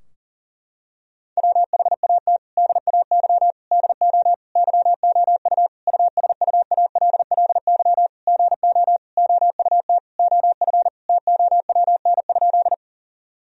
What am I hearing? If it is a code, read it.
Morse code, heard as a keyed tone, about 40 wpm, 700 Hz. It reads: WHAT DAY DO YOU USUALLY GO OUT OF TOWN?
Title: Sentences from Top 1000 Words - Review Part 3 - 40wpm